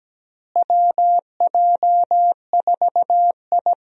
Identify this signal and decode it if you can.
Morse code, heard as a keyed tone, about 17 wpm, 700 Hz. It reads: WJ4I
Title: USA Call Signs - 4 Characters - Repeated 3x - 17wpm